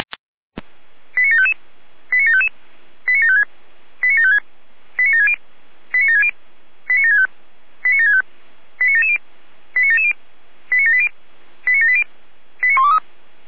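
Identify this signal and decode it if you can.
Five-tone selective call (ZVEI), heard as quick runs of five tones, tones of about 70 ms. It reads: 89755 89755 89756 89756 89760 89760 89765 89765 89770 89770 89780 89780 89123
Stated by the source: ZVEI five-tone selective call 89740 received by SDR.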